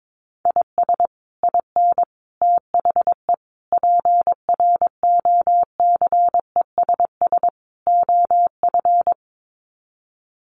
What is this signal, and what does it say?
Morse code, heard as a keyed tone, about 22 wpm, 700 Hz. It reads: IS IN THE PROCESS OF
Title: Sets of 5 Words Encore2 22wpm